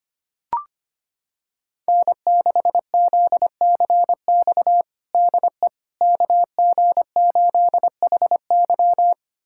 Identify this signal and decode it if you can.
Morse code, heard as a keyed tone, about 25 wpm, 700 Hz. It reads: N6ZCX DE KG8HY
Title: USA Callsign DE Callsign 25wpm